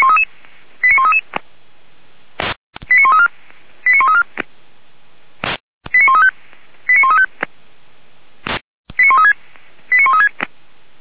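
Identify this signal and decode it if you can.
Five-tone selective call (ZVEI), heard as quick runs of five tones, tones of about 70 ms. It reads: 89133 89134 89134 89135 89135 89136 89136
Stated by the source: ZVEI five-tone call 89132 received by SDR, from feuerwehr-orlamuende.org (fire brigade Orlamünde).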